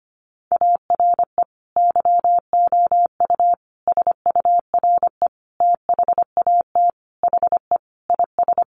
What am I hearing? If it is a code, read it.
Morse code, heard as a keyed tone, about 25 wpm, 700 Hz. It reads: ARE YOU SURE THAT HE IS
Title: Sentences from Top 1500 Words 25wpm